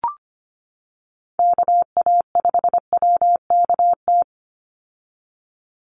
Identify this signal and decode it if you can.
Morse code, heard as a keyed tone, about 25 wpm, 700 Hz. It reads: KA5WKT
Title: USA Call Signs 25wpm